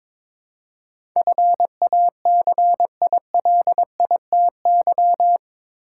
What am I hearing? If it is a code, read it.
Morse code, heard as a keyed tone, about 22 wpm, 700 Hz. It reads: FACILITY
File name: practice / 22wpm / Eight Letter Words - 22wpm